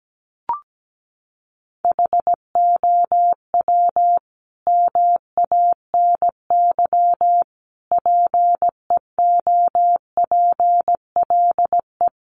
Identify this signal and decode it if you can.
Morse code, heard as a keyed tone, about 17 wpm, 700 Hz. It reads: HOW MANY PEOPLE
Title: Sentences from Top 1200 Words 17wpm